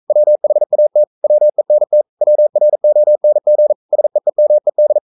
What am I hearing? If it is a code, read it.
Morse code, heard as a keyed tone, about 42 wpm, 600 Hz. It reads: WHAT WENT WRONG SEEMED
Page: Romanen Never af Ken Follet på engelsk.